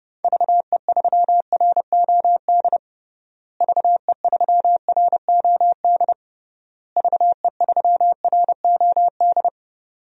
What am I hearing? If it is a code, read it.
Morse code, heard as a keyed tone, about 32 wpm, 700 Hz. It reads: VE3ROD VE3ROD VE3ROD
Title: DXCC-100 Call Signs - 6 Characters - Repeated 3x - 30wpm